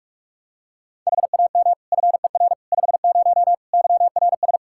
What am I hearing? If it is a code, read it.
Morse code, heard as a keyed tone, about 45 wpm, 700 Hz. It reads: HAM FER 50 YRS
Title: Practice QSO - Exchange 3 - More Info - 45wpm